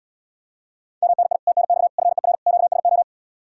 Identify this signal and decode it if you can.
Morse code, heard as a keyed tone, about 74 wpm, 700 Hz. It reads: MNI TNX FER QSO